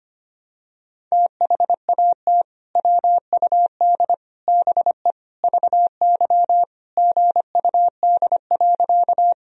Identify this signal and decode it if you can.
Morse code, heard as a keyed tone, about 25 wpm, 700 Hz. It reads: THAT WUD BE VY GUD.
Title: Sentences from Top 200 Words and Abbreviations 25wpm